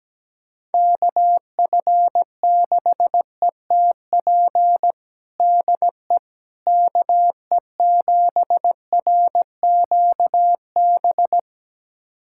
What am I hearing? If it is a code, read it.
Morse code, heard as a keyed tone, about 17 wpm, 700 Hz. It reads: KF6ETP DE KE7RQB
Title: USA Callsign DE Callsign - 6 Characters - 17wpm